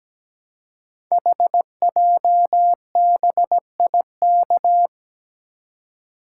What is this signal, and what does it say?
Morse code, heard as a keyed tone, about 17 wpm, 700 Hz. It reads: HJBIK
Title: Code Groups - 5 Letters - 17wpm